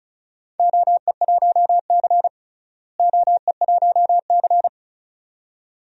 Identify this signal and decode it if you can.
Morse code, heard as a keyed tone, about 36 wpm, 700 Hz. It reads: OE1C OE1C
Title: DXCC-100 Call Signs - 4 Characters - Repeated 2x - 35wpm